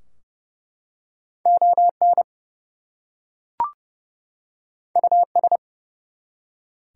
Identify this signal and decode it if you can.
Morse code, heard as a keyed tone, about 30 wpm, 700 Hz. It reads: ON US
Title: Two Letter Words - 30wpm